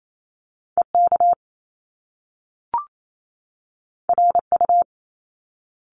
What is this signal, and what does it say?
Morse code, heard as a keyed tone, about 28 wpm, 700 Hz. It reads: EK RU
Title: Top Two Letter Combinations in Words 28wpm